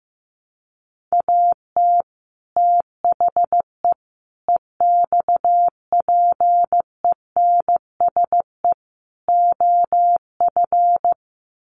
Morse code, heard as a keyed tone, about 15 wpm, 700 Hz. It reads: AT THE EXPENSE OF
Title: Sets of 4 Words 15wpm